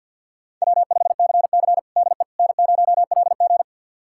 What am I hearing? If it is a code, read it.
Morse code, heard as a keyed tone, about 52 wpm, 700 Hz. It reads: W5CC DE N0LG
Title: USA Callsign DE Callsign - 4 Characters - 50wpm